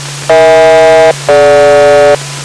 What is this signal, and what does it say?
Aircraft SELCAL, heard as two pairs of tones. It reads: EH-DG